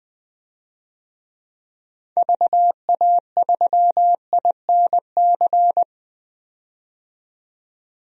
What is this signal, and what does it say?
Morse code, heard as a keyed tone, about 20 wpm, 700 Hz. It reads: VA3INC
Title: DXCC-100 Call Signs - 6 Characters - 20wpm